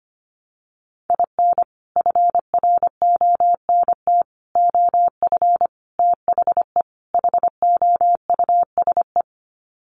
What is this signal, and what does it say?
Morse code, heard as a keyed tone, about 25 wpm, 700 Hz. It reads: IN FRONT OF THE HOUSE
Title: Sets of 5 Words Encore2 25wpm